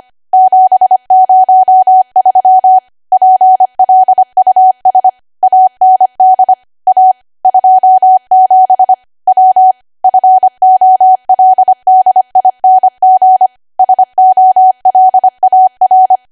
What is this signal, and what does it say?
Morse code, heard as a keyed tone, about 25 wpm, 750 Hz. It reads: 703 PLUS AND A 27 W FOLDING SOLAR
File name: Archive / 25 WPM / 25WPM